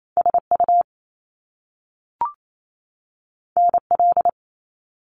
Morse code, heard as a keyed tone, about 28 wpm, 700 Hz. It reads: SU NL
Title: Top Two Letter Combinations in Words 28wpm